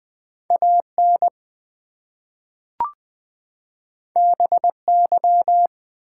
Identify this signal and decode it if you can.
Morse code, heard as a keyed tone, about 20 wpm, 700 Hz. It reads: AN BY
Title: Two Letter Words - 20wpm